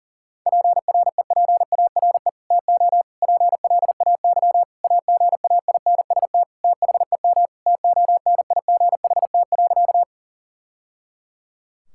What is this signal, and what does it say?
Morse code, heard as a keyed tone, about 37 wpm, 700 Hz. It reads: PREPARE TO PLAY AGAINST THEM TONIGHT.
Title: Sentences from Top 1000 Words - Review Part 6 - 40wpm